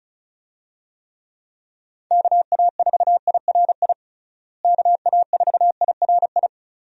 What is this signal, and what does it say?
Morse code, heard as a keyed tone, about 36 wpm, 700 Hz. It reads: KA4IRI KA4IRI
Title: USA Call Signs - 6 Characters - Repeated 2x - 35wpm